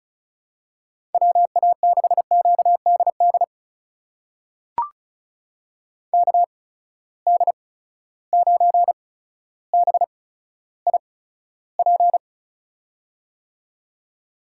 Morse code, heard as a keyed tone, about 36 wpm, spaced as for 15 wpm, 700 Hz. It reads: WA6QDD KD9BIP
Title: USA Call Signs - 6 Characters - 8x Character Spacing - 35wpm